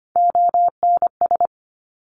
Morse code, heard as a keyed tone, about 25 wpm, 700 Hz. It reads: ONS
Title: Top Three Letter Combinations in Words 25wpm